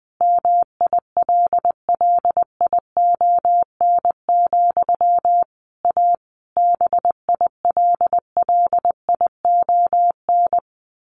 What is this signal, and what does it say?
Morse code, heard as a keyed tone, about 20 wpm, 700 Hz. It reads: MILLION, A BILLION
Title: A Sound Of Thunder 20wpm